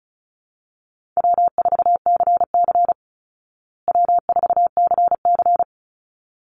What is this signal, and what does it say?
Morse code, heard as a keyed tone, about 34 wpm, 700 Hz. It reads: W4CC W4CC
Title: Easy Call Signs Repeated 2x - 35wpm